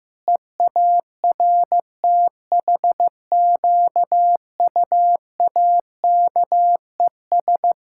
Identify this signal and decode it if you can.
Morse code, heard as a keyed tone, about 15 wpm, 700 Hz. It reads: EARTHQUAKES
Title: Pangram Challenge v2 - 15wpm